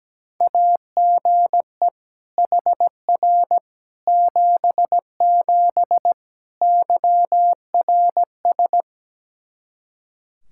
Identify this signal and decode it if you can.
Morse code, heard as a keyed tone, about 17 wpm, 700 Hz. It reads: AGE HR 77 YRS
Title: Practice QSO - Exchange 3 - More Info - 17wpm